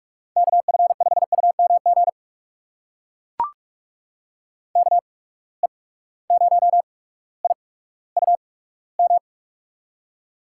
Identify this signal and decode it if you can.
Morse code, heard as a keyed tone, about 44 wpm, spaced as for 19 wpm, 700 Hz. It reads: KF5UMG KE0IUM
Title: USA Call Signs - 6 Characters - 8x Character Spacing - 45wpm